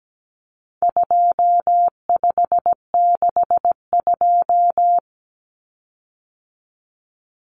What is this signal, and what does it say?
Morse code, heard as a keyed tone, about 17 wpm, 700 Hz. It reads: 2562
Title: Common Cut Numbers - 4-Digit 17wpm